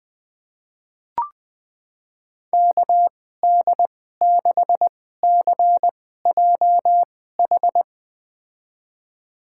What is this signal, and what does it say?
Morse code, heard as a keyed tone, about 20 wpm, spaced as for 14 wpm, 700 Hz. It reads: KD6CJH
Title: USA Call Signs - 6 Characters - 2x Character Spacing - 20wpm